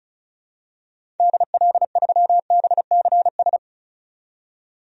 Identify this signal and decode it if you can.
Morse code, heard as a keyed tone, about 36 wpm, 700 Hz. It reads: DL3BCS
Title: DXCC-100 Call Signs - 6 Characters - 35wpm